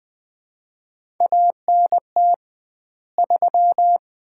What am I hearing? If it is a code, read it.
Morse code, heard as a keyed tone, about 21 wpm, 700 Hz. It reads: ANT 3
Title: Common QSO Phrases - 2x Word-Spacing - 20wpm